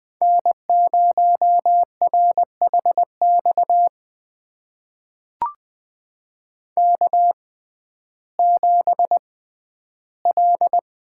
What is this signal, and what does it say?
Morse code, heard as a keyed tone, about 20 wpm, spaced as for 12 wpm, 700 Hz. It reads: N0RHX K7L
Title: USA Call Signs - 5 Characters - 6x Character Spacing - 20wpm